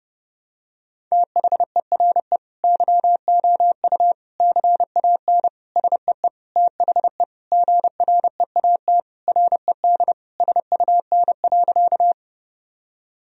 Morse code, heard as a keyed tone, about 30 wpm, 700 Hz. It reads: THERE YOU CAN SEE THE GREAT RED SUN.
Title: Sentences from Top 1000 Words - Review Part 2 - 30wpm